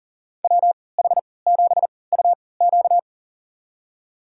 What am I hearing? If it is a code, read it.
Morse code, heard as a keyed tone, about 39 wpm, spaced as for 22 wpm, 700 Hz. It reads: WH7UQ
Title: USA Call Signs - 5 Characters - 3x Character Spacing - 40wpm